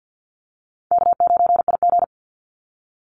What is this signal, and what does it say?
Morse code, heard as a keyed tone, about 50 wpm, 700 Hz. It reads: K9IG
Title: Easy Call Signs - 50wpm